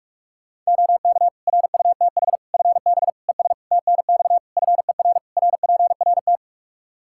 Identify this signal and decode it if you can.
Morse code, heard as a keyed tone, about 46 wpm, 700 Hz. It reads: OK RUTH FB ES TNX FER RPRT
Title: Practice QSO - Exchange 2 - Info - 45wpm